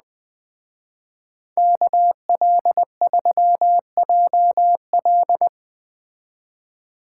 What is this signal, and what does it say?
Morse code, heard as a keyed tone, about 20 wpm, 700 Hz. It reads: KL3JL
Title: Speed Racing - USA 2x2 Call Signs 17wpm